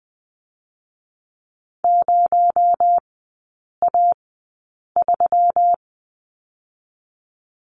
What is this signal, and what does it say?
Morse code, heard as a keyed tone, about 20 wpm, 700 Hz. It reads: 0 A 3
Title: 218 Lesson 35e - Number 0 - Three Characters 20wpm